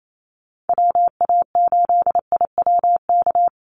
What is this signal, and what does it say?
Morse code, heard as a keyed tone, about 28 wpm, 700 Hz. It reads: WA8IWK
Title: Call Signs - 28wpm